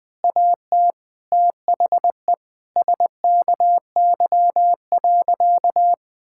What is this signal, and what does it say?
Morse code, heard as a keyed tone, about 20 wpm, 700 Hz. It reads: AT THE SKY.
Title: Sentences with 9-Word Phrases - Spoken Beginning - 20wpm